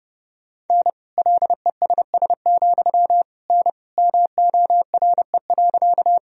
Morse code, heard as a keyed tone, about 30 wpm, 700 Hz. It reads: N LESS, N MORE.
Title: Sentences from Top 400 Words and Abbreviations 30wpm